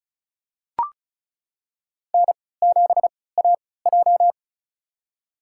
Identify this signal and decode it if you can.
Morse code, heard as a keyed tone, about 34 wpm, spaced as for 20 wpm, 700 Hz. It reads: N7AJ
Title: USA Call Signs - 4 Characters - 3x Character Spacing - 35wpm